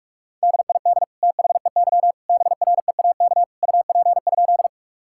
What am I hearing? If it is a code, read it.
Morse code, heard as a keyed tone, about 41 wpm, 700 Hz. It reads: DID THEY BREAK UP?
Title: Sentences from Top 1000 Words - Review Part 6 - 45wpm